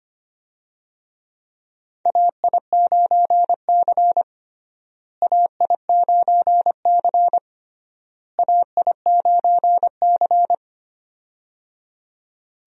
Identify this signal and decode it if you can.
Morse code, heard as a keyed tone, about 25 wpm, 700 Hz. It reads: AI9C AI9C AI9C
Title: USA Call Signs - 4 Characters - Repeated 3x - 25wpm